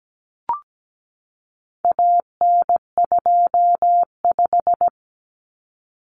Morse code, heard as a keyed tone, about 17 wpm, 700 Hz. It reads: AN25
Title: Common Cut Numbers - 4-Digit 17wpm